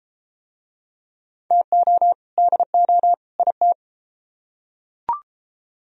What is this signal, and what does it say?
Morse code, heard as a keyed tone, about 33 wpm, 700 Hz. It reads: TO DO IT